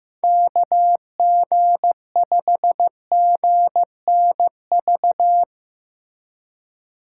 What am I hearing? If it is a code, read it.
Morse code, heard as a keyed tone, about 15 wpm, 700 Hz. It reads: KG5GNV
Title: USA Call Signs - 6 Characters - 3x Character Spacing - 15wpm